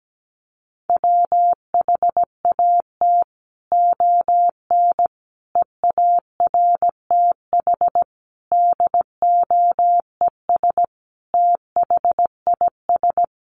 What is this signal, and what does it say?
Morse code, heard as a keyed tone, about 17 wpm, 700 Hz. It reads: WHAT ON EARTH DOES THIS
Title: Sentences from Top 200 Words 17wpm